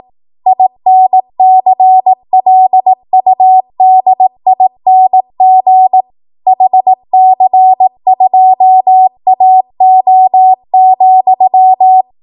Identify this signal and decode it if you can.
Morse code, heard as a keyed tone, about 18 wpm, 750 Hz. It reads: INCLUDING HC2AO,
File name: Archive / 18 WPM / 18WPM